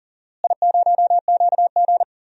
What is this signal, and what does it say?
Morse code, heard as a keyed tone, about 40 wpm, 700 Hz. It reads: I0QG
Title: DXCC-100 Call Signs - 4 Characters - Repeated 2x - 40wpm